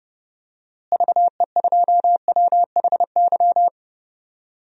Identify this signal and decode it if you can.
Morse code, heard as a keyed tone, about 29 wpm, 700 Hz. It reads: VE2WHY
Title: DXCC-100 Call Signs - 6 Characters - Repeated 3x - 30wpm